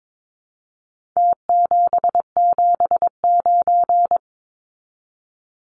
Morse code, heard as a keyed tone, about 22 wpm, 700 Hz. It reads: T779
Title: Numbers - Contest Style - 4-Digit 22wpm